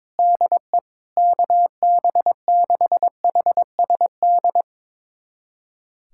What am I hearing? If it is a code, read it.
Morse code, heard as a keyed tone, about 22 wpm, 700 Hz. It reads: DE KB6HSD
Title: USA Callsign DE Callsign 22wpm